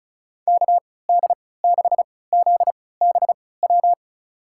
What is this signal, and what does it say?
Morse code, heard as a keyed tone, about 34 wpm, spaced as for 20 wpm, 700 Hz. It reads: KD6ZBW
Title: USA Call Signs - 6 Characters - 3x Character Spacing - 35wpm